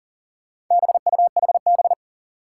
Morse code, heard as a keyed tone, about 44 wpm, 700 Hz. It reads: BUHB
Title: Four Challenging Letter-Number Combinations 40wpm